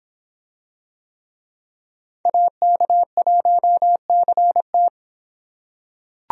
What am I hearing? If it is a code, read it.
Morse code, heard as a keyed tone, about 25 wpm, 700 Hz. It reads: AK1CT